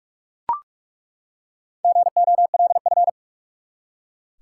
Morse code, heard as a keyed tone, about 42 wpm, 700 Hz. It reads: GOLF
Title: Four Letter Words - 45wpm